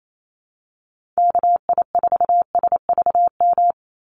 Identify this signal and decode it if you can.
Morse code, heard as a keyed tone, about 28 wpm, 700 Hz. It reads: KI4SVM
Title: NA SOTA Activator Callsigns 28wpm